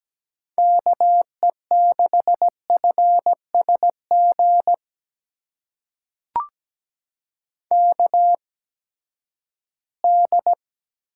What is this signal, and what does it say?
Morse code, heard as a keyed tone, about 17 wpm, spaced as for 12 wpm, 700 Hz. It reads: KE6FSG KD
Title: USA Call Signs - 8x Character Spacing - 17wpm